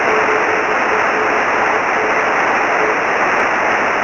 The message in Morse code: none

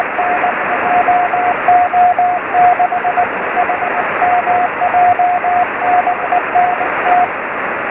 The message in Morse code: none

